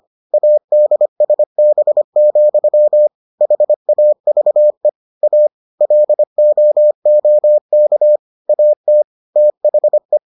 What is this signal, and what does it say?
Morse code, heard as a keyed tone, about 25 wpm, 600 Hz. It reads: ADSB, HAVE A LOOK AT THE